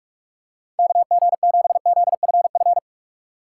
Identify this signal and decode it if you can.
Morse code, heard as a keyed tone, about 44 wpm, 700 Hz. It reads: KG7ZFF